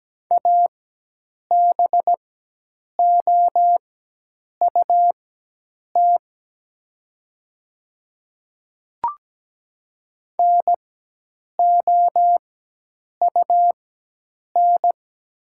Morse code, heard as a keyed tone, about 17 wpm, spaced as for 8 wpm, 700 Hz. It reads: ABOUT NOUN
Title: 4x Character Spacing - Top 500 Words - 17wpm